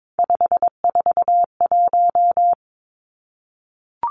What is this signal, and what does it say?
Morse code, heard as a keyed tone, about 22 wpm, 700 Hz. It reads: 541
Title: Numbers - 3-Digit 22wpm